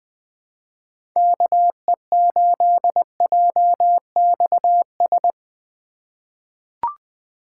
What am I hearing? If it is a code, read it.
Morse code, heard as a keyed tone, about 20 wpm, 700 Hz. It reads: KE8JXS
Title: USA Call Signs - 6 Characters - 3x Character Spacing - 20wpm